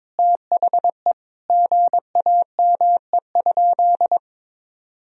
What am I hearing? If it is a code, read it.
Morse code, heard as a keyed tone, about 23 wpm, 700 Hz. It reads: THE GAME?
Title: Sentences from Top 1000 Words - Review Part 4 - 22wpm